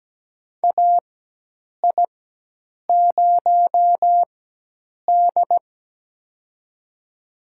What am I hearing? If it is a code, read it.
Morse code, heard as a keyed tone, about 17 wpm, spaced as for 8 wpm, 700 Hz. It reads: AI0D
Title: USA Call Signs - 4 Characters - 4x Character Spacing - 17wpm